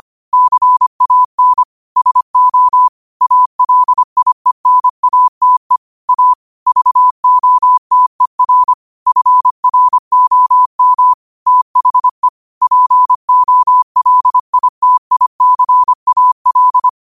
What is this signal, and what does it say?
Morse code, heard as a keyed tone, about 25 wpm, 1000 Hz. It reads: CAN SO ALIENATE A VOTER FROM THE POLITICAL
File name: qotdfiles / 25wpm